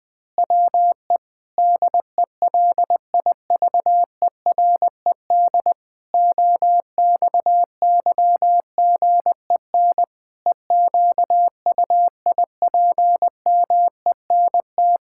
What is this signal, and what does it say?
Morse code, heard as a keyed tone, about 20 wpm, 700 Hz. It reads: WE DELIVERED OXYGEN EQUIPMENT
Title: Pangram Challenge v2 - 20wpm